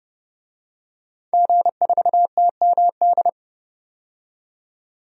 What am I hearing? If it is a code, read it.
Morse code, heard as a keyed tone, about 30 wpm, 700 Hz. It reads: G4TMD
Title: DXCC-100 Call Signs - 5 Characters - 30wpm